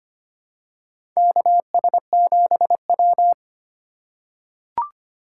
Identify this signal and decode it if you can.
Morse code, heard as a keyed tone, about 25 wpm, 700 Hz. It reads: KS7W
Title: USA Call Signs - 4 Characters - 2x Character Spacing - 25wpm